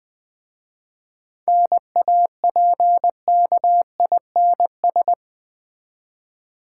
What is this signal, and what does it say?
Morse code, heard as a keyed tone, about 20 wpm, 700 Hz. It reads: NAPKINS
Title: Easy Words with Hints - 20wpm